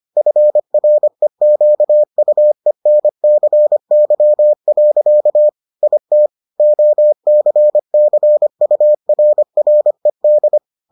Morse code, heard as a keyed tone, about 25 wpm, 600 Hz. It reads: FREQUENCY. IT OCCURRED